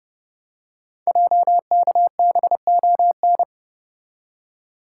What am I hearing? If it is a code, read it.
Morse code, heard as a keyed tone, about 30 wpm, 700 Hz. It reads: JKBON
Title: Code Groups - 5 Letter:Numbers - 30wpm